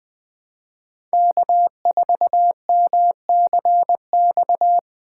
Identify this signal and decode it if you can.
Morse code, heard as a keyed tone, about 20 wpm, 700 Hz. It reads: K4MCX
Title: Speed Racing - USA 1x3 Call Signs 17wpm